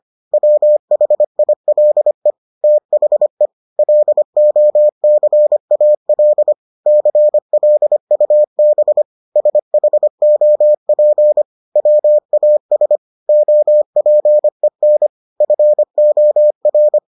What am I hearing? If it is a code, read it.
Morse code, heard as a keyed tone, about 25 wpm, 600 Hz. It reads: WHILE THE LOCAL CLUB SHOP WAS OPEN FOR